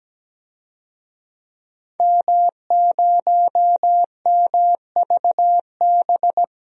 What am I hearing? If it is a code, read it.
Morse code, heard as a keyed tone, about 17 wpm, 700 Hz. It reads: M0MVB
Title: DXCC-100 Call Signs - 5 Characters - 17wpm